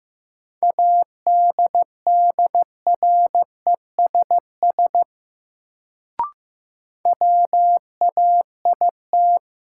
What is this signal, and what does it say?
Morse code, heard as a keyed tone, about 15 wpm, 700 Hz. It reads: ADDRESS WAIT
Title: Sets of 5 Words Encore - New Word Drill - 15wpm